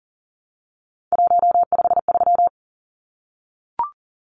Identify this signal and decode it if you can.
Morse code, heard as a keyed tone, about 39 wpm, 700 Hz. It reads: 153
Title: Numbers - 3-Digit 40wpm